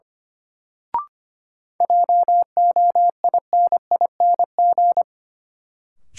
Morse code, heard as a keyed tone, about 25 wpm, 700 Hz. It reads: JOINING
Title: Sets of 5 Words - New Word Drill - 25wpm